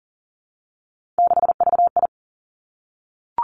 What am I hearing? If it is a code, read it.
Morse code, heard as a keyed tone, about 38 wpm, 700 Hz. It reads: 6VI
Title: Three Challenging Letter-Number Combinations 40wpm